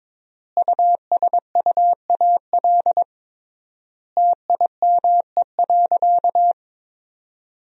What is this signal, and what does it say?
Morse code, heard as a keyed tone, about 22 wpm, 700 Hz. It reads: USUAL TIME.
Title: 3x Word-Spacing - Sentences from Top 400 Words - 22wpm